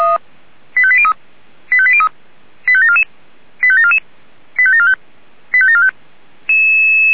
Five-tone selective call (ZVEI), heard as quick runs of five tones, tones of about 70 ms. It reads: 85803 85803 85744 85744 85745 85745
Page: ZVEI